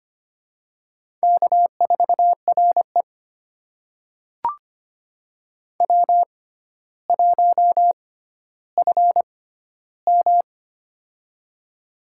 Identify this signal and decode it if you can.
Morse code, heard as a keyed tone, about 25 wpm, spaced as for 13 wpm, 700 Hz. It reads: K4RE W1FM
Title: USA Call Signs - 4 Characters - 6x Character Spacing - 25wpm